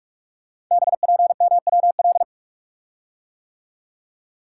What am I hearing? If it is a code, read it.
Morse code, heard as a keyed tone, about 44 wpm, 690 Hz. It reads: BPMWL
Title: Code Groups - 5 Letters - 45wpm